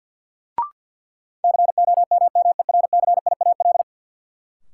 Morse code, heard as a keyed tone, about 52 wpm, 700 Hz. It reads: COMMERCIAL
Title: Sentences from Top 1000 Words - Review Part 5 - New Word Drill - 50wpm